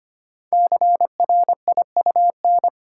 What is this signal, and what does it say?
Morse code, heard as a keyed tone, about 25 wpm, 700 Hz. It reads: CRIUN
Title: Code Groups - 5 Letters - 25wpm